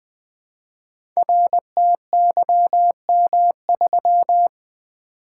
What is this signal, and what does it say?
Morse code, heard as a keyed tone, about 20 wpm, 700 Hz. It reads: RTYM3
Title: Code Groups - 5 Letter:Numbers - 20wpm